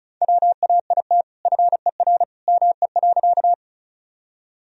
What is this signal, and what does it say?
Morse code, heard as a keyed tone, about 34 wpm, 700 Hz. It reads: WAIT FER ME.
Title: Sentences from Top 500 Words and Abbreviations 35wpm